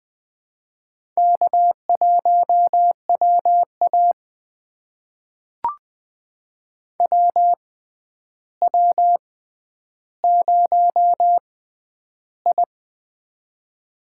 Morse code, heard as a keyed tone, about 20 wpm, spaced as for 10 wpm, 700 Hz. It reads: K1WA WW0I
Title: USA Call Signs - 4 Characters - 6x Character Spacing - 20wpm